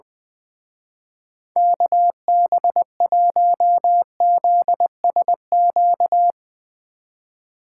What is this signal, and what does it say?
Morse code, heard as a keyed tone, about 22 wpm, 700 Hz. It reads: KB1ZSQ
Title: Speed Racing - USA 2x3 Call Signs 20wpm